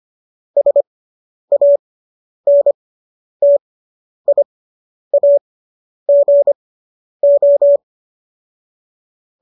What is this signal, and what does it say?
Morse code, heard as a keyed tone, about 25 wpm, spaced as for 10 wpm, 570 Hz. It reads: SANTIAGO